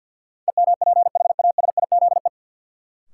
Morse code, heard as a keyed tone, about 53 wpm, 700 Hz. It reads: EMPHASIZE
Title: Sentences from Top 1500 Words - New Word Drill - 50wpm